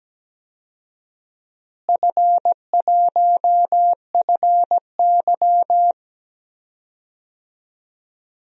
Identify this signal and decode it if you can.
Morse code, heard as a keyed tone, about 17 wpm, 700 Hz. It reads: F1FY
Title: DXCC-100 Call Signs - 4 Characters - Repeated 3x - 17wpm